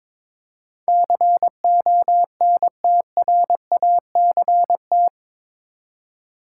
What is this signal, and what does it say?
Morse code, heard as a keyed tone, about 22 wpm, 700 Hz. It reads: CONTRACT
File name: practice / 22wpm / Eight Letter Words - 22wpm